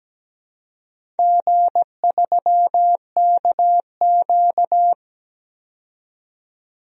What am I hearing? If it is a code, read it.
Morse code, heard as a keyed tone, about 17 wpm, 700 Hz. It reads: G3KQ
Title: DXCC-100 Call Signs - 4 Characters - 17wpm